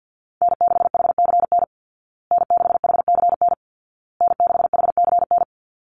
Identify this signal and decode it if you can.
Morse code, heard as a keyed tone, about 50 wpm, 700 Hz. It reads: N6HCN N6HCN N6HCN
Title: Call Signs Repeated 2x - 50wpm